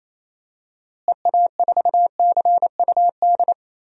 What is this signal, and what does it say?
Morse code, heard as a keyed tone, about 27 wpm, 700 Hz. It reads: EA4CUD
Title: DXCC-100 Call Signs - 6 Characters - Repeated 2x - 28wpm